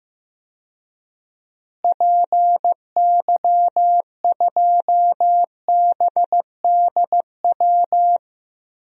Morse code, heard as a keyed tone, about 15 wpm, 700 Hz. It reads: PY2BDW
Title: DXCC-100 Call Signs - 6 Characters - Repeated 3x - 15wpm